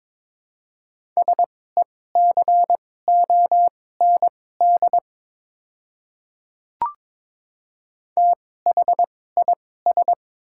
Morse code, heard as a keyed tone, about 22 wpm, spaced as for 16 wpm, 700 Hz. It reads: SECOND THIS
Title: 2x Character Spacing - Top 300 Words - 22wpm